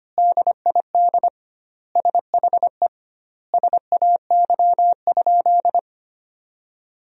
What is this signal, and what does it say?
Morse code, heard as a keyed tone, about 25 wpm, 700 Hz. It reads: DID SHE SAY?
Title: 2x Word-Spacing - Sentences from Top 500 Words - 25wpm